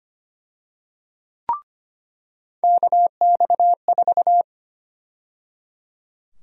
Morse code, heard as a keyed tone, about 25 wpm, 700 Hz. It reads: KX4
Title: USA Call Sign Prefixes - 25wpm